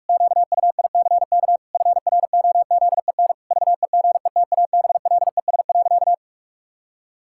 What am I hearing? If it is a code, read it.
Morse code, heard as a keyed tone, about 46 wpm, 700 Hz. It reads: QUICK FROZEN VEGETABLES.
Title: Pangram Challenge v2 - 45wpm